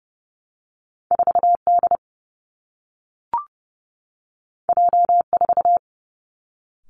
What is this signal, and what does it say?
Morse code, heard as a keyed tone, about 30 wpm, 700 Hz. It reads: 4 D J 4
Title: Two Challenging Letter-Number Combinations 30wpm